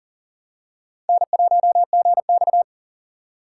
Morse code, heard as a keyed tone, about 40 wpm, 700 Hz. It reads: N1GX
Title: USA Call Signs - 4 Characters - 2x Character Spacing - 40wpm